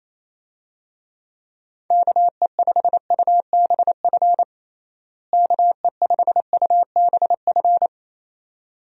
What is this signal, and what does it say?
Morse code, heard as a keyed tone, about 27 wpm, 700 Hz. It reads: KE5UBF KE5UBF
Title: USA Call Signs - 6 Characters - Repeated 2x - 28wpm